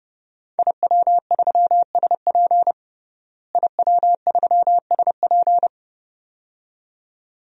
Morse code, heard as a keyed tone, about 30 wpm, 700 Hz. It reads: IW3SP IW3SP
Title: DXCC-100 Call Signs - 5 Characters - Repeated 2x - 30wpm